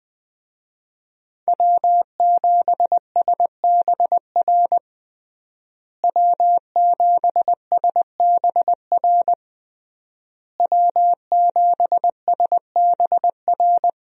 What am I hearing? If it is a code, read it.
Morse code, heard as a keyed tone, about 20 wpm, 700 Hz. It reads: W7SBR W7SBR W7SBR
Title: USA Call Signs - 5 Characters - Repeated 3x - 20wpm